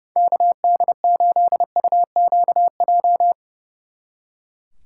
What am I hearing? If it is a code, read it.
Morse code, heard as a keyed tone, about 35 wpm, 700 Hz. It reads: KD8UQJ
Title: Speed Racing - USA 2x3 Call Signs 30wpm